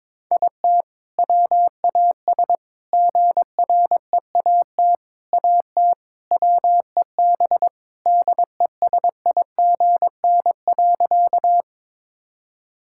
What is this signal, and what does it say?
Morse code, heard as a keyed tone, about 22 wpm, 690 Hz. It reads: IT WAS GREAT AT WEB DESIGN.
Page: morse-code-ninja-practice-22wpm